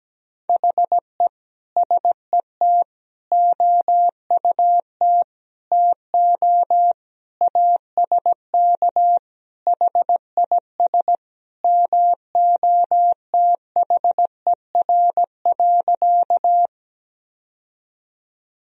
Morse code, heard as a keyed tone, about 17 wpm, 700 Hz. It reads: HE SET OUT TO ASK HIS MOTHER.
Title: Sentences from Top 1000 Words - Review Part 1 - 17wpm